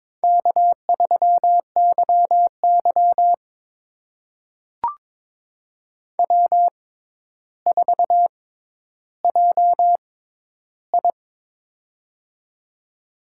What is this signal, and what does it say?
Morse code, heard as a keyed tone, about 22 wpm, spaced as for 11 wpm, 700 Hz. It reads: K3YY W4JI
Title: USA Call Signs - 4 Characters - 6x Character Spacing - 22wpm